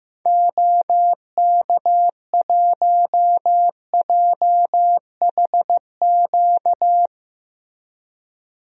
Morse code, heard as a keyed tone, about 15 wpm, 700 Hz. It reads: OK1JHQ
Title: DXCC-100 Call Signs - 6 Characters - Repeated 2x - 15wpm